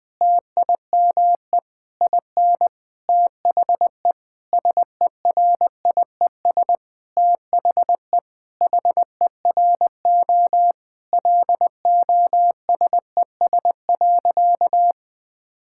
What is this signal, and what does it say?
Morse code, heard as a keyed tone, about 21 wpm, 700 Hz. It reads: TIME IN THE SERIES THE HERO LOSES.
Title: Sentences with 6-Word Phrases - Spoken Beginning - 20wpm